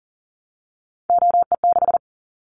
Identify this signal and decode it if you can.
Morse code, heard as a keyed tone, about 38 wpm, 690 Hz. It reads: OE6
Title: Callsign Prefixes for Top 100 DXCC - 40wpm